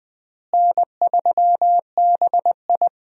Morse code, heard as a keyed tone, about 20 wpm, 700 Hz. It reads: N3BI
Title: USA Call Signs - 4 Characters - Repeated 2x - 20wpm